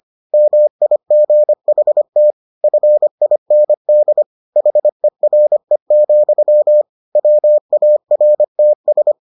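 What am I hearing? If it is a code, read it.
Morse code, heard as a keyed tone, about 25 wpm, 600 Hz. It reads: MIGHT FIND HERE, WARTS
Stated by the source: Morse edition of the Foundations of Amateur Radio episodes.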